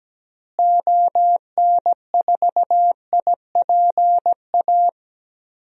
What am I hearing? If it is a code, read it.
Morse code, heard as a keyed tone, about 17 wpm, 700 Hz. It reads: ON4IPA
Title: DXCC-100 Call Signs - 6 Characters - Repeated 3x - 17wpm